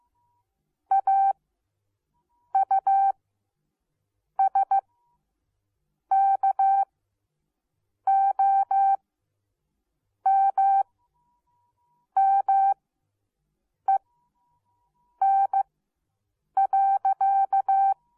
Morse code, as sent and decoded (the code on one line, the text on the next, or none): .- ..- ... -.- --- -- -- . -. .-.-.-
AUSKOMMEN.